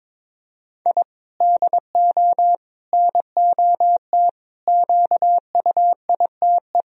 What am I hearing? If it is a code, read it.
Morse code, heard as a keyed tone, about 22 wpm, 700 Hz. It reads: I DO NOT QUITE
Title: Sentences from Top 1000 Words - Review Part 1 - 22wpm